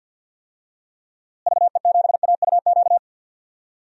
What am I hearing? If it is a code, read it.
Morse code, heard as a keyed tone, about 51 wpm, 700 Hz. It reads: VE7AUQ
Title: DXCC-100 Call Signs - 6 Characters - Repeated 3x - 50wpm